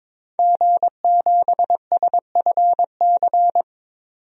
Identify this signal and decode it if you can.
Morse code, heard as a keyed tone, about 22 wpm, 700 Hz. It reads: G7SFC
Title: DXCC-100 Call Signs - 5 Characters - 22wpm